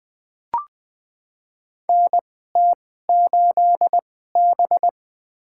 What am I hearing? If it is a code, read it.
Morse code, heard as a keyed tone, about 20 wpm, spaced as for 14 wpm, 700 Hz. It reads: NT8B
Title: USA Call Signs - 2x Character Spacing - 20wpm